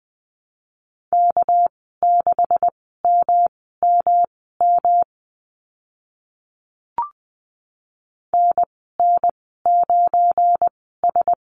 Morse code, heard as a keyed tone, about 20 wpm, spaced as for 14 wpm, 700 Hz. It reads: K6MMM NN9S
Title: 213 Lesson 34g - Number 8 - Call Signs 20wpm